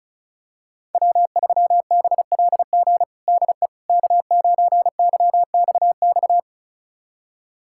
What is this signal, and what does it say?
Morse code, heard as a keyed tone, about 36 wpm, 690 Hz. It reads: W3BLG DE K9YXX
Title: USA Callsign DE Callsign - 5 Characters - 35wpm